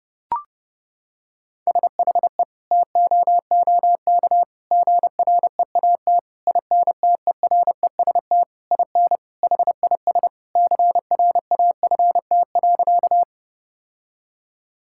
Morse code, heard as a keyed tone, about 30 wpm, 700 Hz. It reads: SHE TOOK GREAT INTEREST IN HIS CRAFT.
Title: Sentences from Top 1000 Words - Review Part 6 - 30wpm